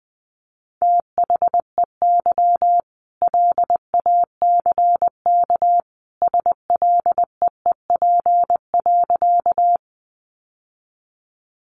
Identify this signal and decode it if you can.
Morse code, heard as a keyed tone, about 20 wpm, 700 Hz. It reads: THEY LACK SLEEP.
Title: Sentences from Top 1100 Words 20wpm